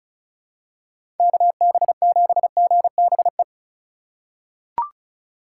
Morse code, heard as a keyed tone, about 34 wpm, 700 Hz. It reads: KB7GBE
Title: USA Call Signs - 6 Characters - 3x Character Spacing - 35wpm